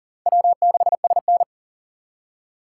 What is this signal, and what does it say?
Morse code, heard as a keyed tone, about 40 wpm, 700 Hz. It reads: W6SN
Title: USA Call Signs - 4 Characters - 4x Character Spacing - 40wpm